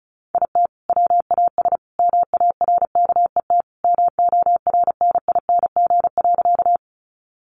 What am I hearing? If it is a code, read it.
Morse code, heard as a keyed tone, about 34 wpm, 700 Hz. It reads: IT WAS MARKET MORNING.
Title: Sentences from Top 600 Words 35wpm